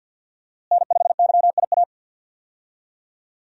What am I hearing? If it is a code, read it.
Morse code, heard as a keyed tone, about 51 wpm, 700 Hz. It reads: N5YIA
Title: USA Call Signs - 5 Characters - Repeated 3x - 50wpm